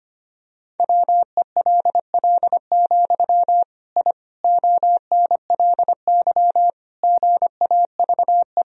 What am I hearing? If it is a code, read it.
Morse code, heard as a keyed tone, about 25 wpm, 700 Hz. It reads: WELL, I ONLY GAVE
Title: Sentences from Top 1000 Words - Review Part 2 - 25wpm